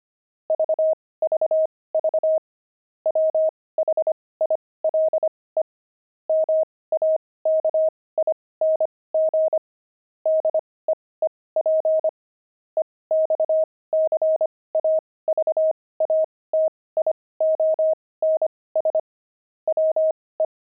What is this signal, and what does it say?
Morse code, heard as a keyed tone, about 25 wpm, spaced as for 18 wpm, 630 Hz. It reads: VVV WHILE MAKING DEEP EXCAVATIONS WE
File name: pangram-fs-18-cs-25-deep